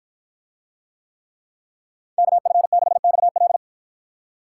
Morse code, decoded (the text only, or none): KF6XL